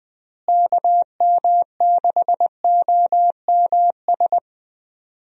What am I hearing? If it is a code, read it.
Morse code, heard as a keyed tone, about 20 wpm, 700 Hz. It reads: KM6OMS
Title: USA Call Signs - 6 Characters - Repeated 2x - 20wpm